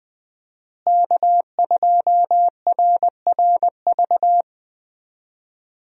Morse code, heard as a keyed tone, about 20 wpm, 700 Hz. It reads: K2RRV
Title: USA Call Signs - 5 Characters - Repeated 3x - 20wpm